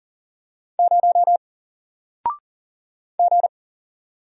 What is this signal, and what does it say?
Morse code, heard as a keyed tone, about 40 wpm, 700 Hz. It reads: 0 G
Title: Single Letter-Number Alt 40wpm